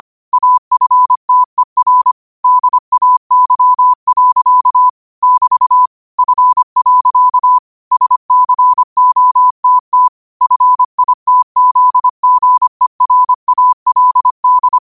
Morse code, heard as a keyed tone, about 25 wpm, 1000 Hz. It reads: AFTER DAY. <BT> F. SCOTT FITZGERALD